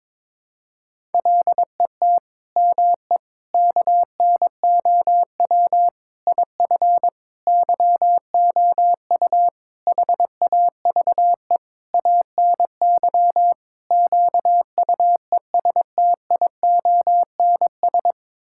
Morse code, heard as a keyed tone, about 22 wpm, 700 Hz. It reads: LET ME KNOW IF YOU HAVE ANY QUESTIONS
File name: practice / 22wpm / Sets of 8 Words - 22wpm